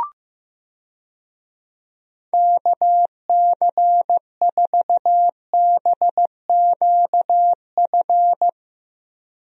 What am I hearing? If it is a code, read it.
Morse code, heard as a keyed tone, about 15 wpm, 700 Hz. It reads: KC4BQF
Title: USA Call Signs - 6 Characters - Repeated 3x - 15wpm